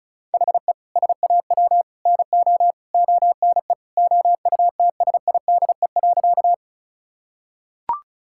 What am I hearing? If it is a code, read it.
Morse code, heard as a keyed tone, about 36 wpm, 700 Hz. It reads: HE SAW NO ONE OUTSIDE.
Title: Sentences from Top 1000 Words - Review Part 6 - 35wpm